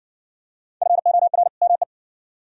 Morse code, heard as a keyed tone, about 55 wpm, 700 Hz. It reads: FOR ME